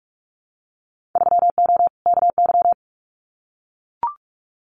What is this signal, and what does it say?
Morse code, heard as a keyed tone, about 44 wpm, 700 Hz. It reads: 3O KY
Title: Winter Field Day Exchanges 45wpm